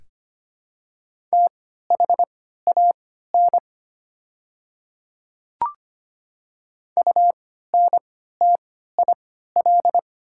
Morse code, heard as a keyed tone, about 25 wpm, spaced as for 14 wpm, 700 Hz. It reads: THAN UNTIL
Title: 3x Character Spacing - Top 300 Words - 25wpm